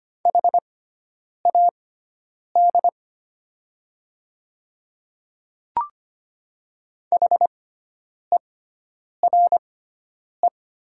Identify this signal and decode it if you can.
Morse code, heard as a keyed tone, about 25 wpm, spaced as for 9 wpm, 700 Hz. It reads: HAD HERE
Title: 6x Character Spacing - Top 200 Words - 25wpm